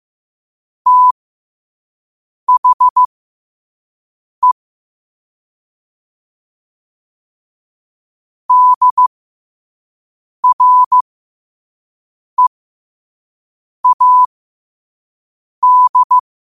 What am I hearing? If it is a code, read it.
Morse code, heard as a keyed tone, about 15 wpm, spaced as for 5 wpm, 1000 Hz. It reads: THE DREAD
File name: qotdfiles / 5wpm